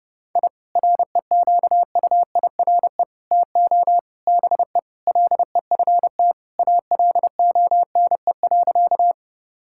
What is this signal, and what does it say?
Morse code, heard as a keyed tone, about 32 wpm, 700 Hz. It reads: I REQUIRE TO BE LEFT ALONE.
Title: Sentences from Top 1000 Words - Review Part 6 - 30wpm